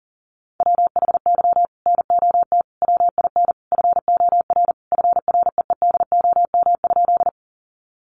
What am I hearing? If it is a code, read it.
Morse code, heard as a keyed tone, about 40 wpm, 700 Hz. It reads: WHY NOT WIN FOR FREEDOM?
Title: Sentences from Top 1500 Words 40wpm